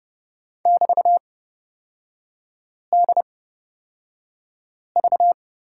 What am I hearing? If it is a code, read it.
Morse code, heard as a keyed tone, about 31 wpm, spaced as for 5 wpm, 700 Hz. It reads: <BT>DV
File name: Lesson 36e - BT - Three Characters 30wpm